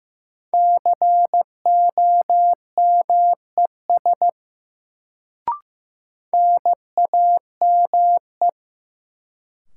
Morse code, heard as a keyed tone, about 15 wpm, 700 Hz. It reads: COMES NAME
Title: Sets of 3 Words - New Word Drill - 15wpm